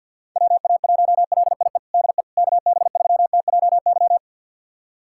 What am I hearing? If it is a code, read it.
Morse code, heard as a keyed tone, about 48 wpm, 700 Hz. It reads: WA1LIE DE KB3TJY
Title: USA Callsign DE Callsign - 6 Characters - 50wpm